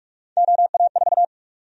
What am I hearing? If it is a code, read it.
Morse code, heard as a keyed tone, about 46 wpm, 700 Hz. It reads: OA4
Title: USA Call Sign Prefixes - Repeated 2x - 45wpm